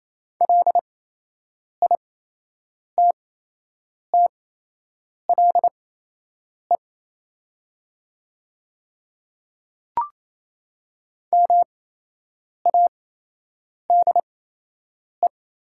Morse code, heard as a keyed tone, about 27 wpm, spaced as for 8 wpm, 690 Hz. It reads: LITTLE MADE
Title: 8x Character Spacing - Top 300 Words - 28wpm